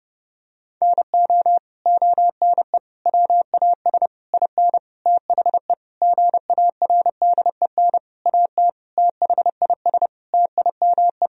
Morse code, heard as a keyed tone, about 28 wpm, 700 Hz. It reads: NO ONE WAS IN THE GARDEN AT THIS TIME
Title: Sentences from Top 1000 Words - Review Part 4 - 30wpm